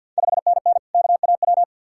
Morse code, heard as a keyed tone, about 52 wpm, 700 Hz. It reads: 5NN KAW
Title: Ontario QSO Party Exchange 50wpm